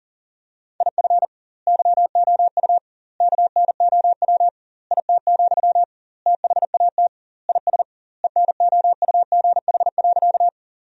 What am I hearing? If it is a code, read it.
Morse code, heard as a keyed tone, about 40 wpm, 700 Hz. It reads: IF YOU KNOW IT, THAT IS ENOUGH.
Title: 2x Word-Spacing - Sentences from Top 400 Words - 40wpm